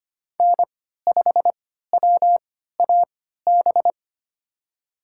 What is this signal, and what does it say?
Morse code, heard as a keyed tone, about 25 wpm, spaced as for 14 wpm, 700 Hz. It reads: N5WAB
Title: USA Call Signs - 3x Character Spacing - 25wpm